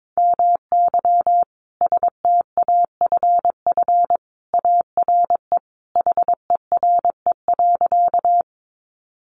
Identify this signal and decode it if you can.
Morse code, heard as a keyed tone, about 22 wpm, 700 Hz. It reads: MY STAFF ARE HERE.
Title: Sentences from Top 1200 Words 22wpm